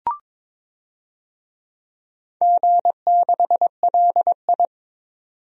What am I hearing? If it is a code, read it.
Morse code, heard as a keyed tone, about 22 wpm, 700 Hz. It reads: G6LI
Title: DXCC-100 Call Signs - 4 Characters - 22wpm